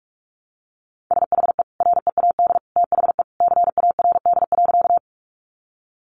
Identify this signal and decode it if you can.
Morse code, heard as a keyed tone, about 44 wpm, 700 Hz. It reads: SHE READ THE CARD.
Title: Sentences from Top 600 Words 45wpm